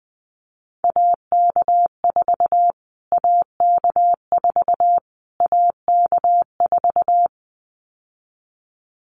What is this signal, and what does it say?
Morse code, heard as a keyed tone, about 20 wpm, 700 Hz. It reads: AK4 AK4 AK4
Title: Callsign Prefixes for Top 25 DXCC - Repeated 3x - 20wpm